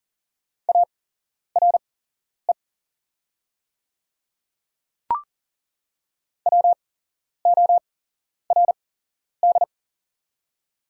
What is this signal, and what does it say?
Morse code, heard as a keyed tone, about 38 wpm, spaced as for 11 wpm, 700 Hz. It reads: ARE WORD
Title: 8x Character Spacing - Top 100 Words - 40wpm